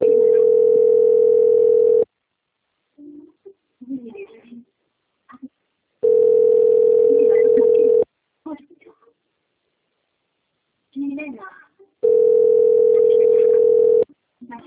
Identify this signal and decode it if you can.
Ringback tone (North America), 440 + 480 Hz, with speech.
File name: out- - - - - .48982